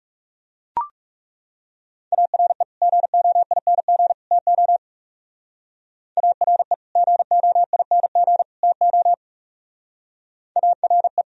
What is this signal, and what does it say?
Morse code, heard as a keyed tone, about 37 wpm, 700 Hz. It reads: ARE GOING TO ARE GOING TO ARE
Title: Speed-Racing - Sets of 3 Words 30wpm